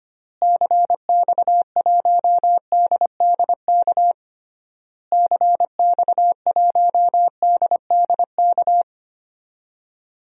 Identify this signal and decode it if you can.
Morse code, heard as a keyed tone, about 25 wpm, 700 Hz. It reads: CX1DDK CX1DDK
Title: DXCC-100 Call Signs - 6 Characters - Repeated 3x - 25wpm